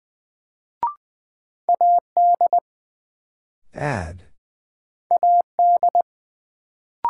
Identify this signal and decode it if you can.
Morse code, heard as a keyed tone, about 20 wpm, 700 Hz. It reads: AD AD
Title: Sentences from Top 1300 Words - New Word Drill - 20wpm